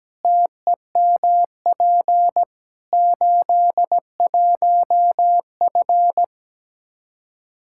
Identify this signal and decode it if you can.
Morse code, heard as a keyed tone, about 17 wpm, 700 Hz. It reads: TEMP 81F
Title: Practice QSO - Exchange 2 - Info - 17wpm